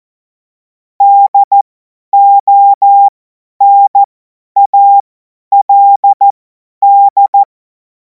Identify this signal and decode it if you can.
Morse code, heard as a keyed tone, about 14 wpm, spaced as for 10 wpm, 800 Hz. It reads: DONALD
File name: Morsekurs / Witze / Witz 12. Mai 2026 50 F70